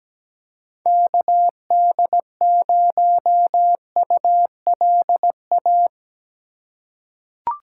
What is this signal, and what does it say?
Morse code, heard as a keyed tone, about 17 wpm, 700 Hz. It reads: KD0ULA
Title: USA Call Signs - 6 Characters - 3x Character Spacing - 17wpm